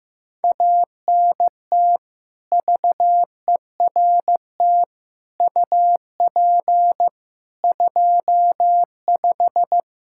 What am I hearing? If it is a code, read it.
Morse code, heard as a keyed tone, about 15 wpm, 700 Hz. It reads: ANT VERT UP 25
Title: K1USN SST QSO Practice 15wpm